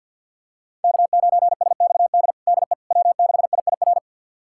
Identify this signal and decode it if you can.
Morse code, heard as a keyed tone, about 52 wpm, 700 Hz. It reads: K9SXD DE W6IIR
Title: USA Callsign DE Callsign - 5 Characters - 50wpm